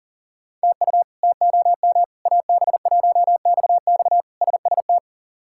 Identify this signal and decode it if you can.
Morse code, heard as a keyed tone, about 40 wpm, 700 Hz. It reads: TU TOM AB1XX SST
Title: K1USN SST QSO Practice - 40wpm